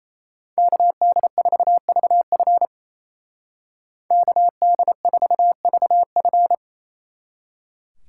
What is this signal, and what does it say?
Morse code, heard as a keyed tone, about 34 wpm, 700 Hz. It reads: KD4VF KD4VF
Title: Speed Racing - USA 2x2 Call Signs 28wpm